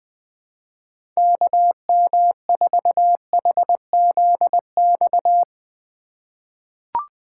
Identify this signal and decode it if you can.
Morse code, heard as a keyed tone, about 20 wpm, 700 Hz. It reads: KM4HZX
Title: USA Call Signs - 6 Characters - 2x Character Spacing - 20wpm